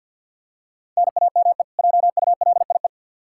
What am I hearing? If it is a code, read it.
Morse code, heard as a keyed tone, about 52 wpm, 700 Hz. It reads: NAME JULIE
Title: K1USN SST QSO Practice 50wpm